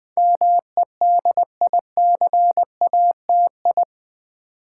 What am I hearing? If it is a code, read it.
Morse code, heard as a keyed tone, about 20 wpm, 700 Hz. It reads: MEDICATI
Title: Build A Word - 4 Letters At A Time - 10-Letter Words - 20wpm